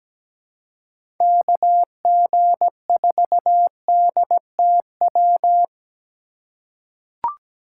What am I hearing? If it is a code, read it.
Morse code, heard as a keyed tone, about 17 wpm, 700 Hz. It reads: KG4DTW
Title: USA Call Signs - 6 Characters - 3x Character Spacing - 17wpm